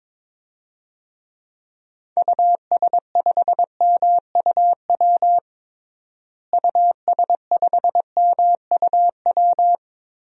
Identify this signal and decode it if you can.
Morse code, heard as a keyed tone, about 22 wpm, 700 Hz. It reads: US5MUW US5MUW
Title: DXCC-100 Call Signs - 6 Characters - Repeated 3x - 22wpm